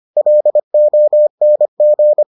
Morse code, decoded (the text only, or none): LONG